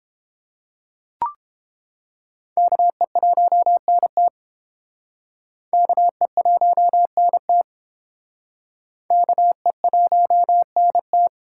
Morse code, heard as a keyed tone, about 30 wpm, 700 Hz. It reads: KE1NT KE1NT KE1NT